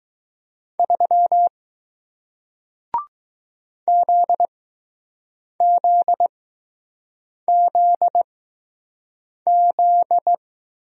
Morse code, heard as a keyed tone, about 20 wpm, spaced as for 6 wpm, 700 Hz. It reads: 3 ZZZZ